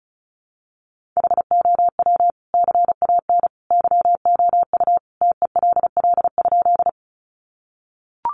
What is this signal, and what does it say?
Morse code, heard as a keyed tone, about 34 wpm, 700 Hz. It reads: HOW CAN YOU TELL?
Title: Sentences from Top 200 Words 35wpm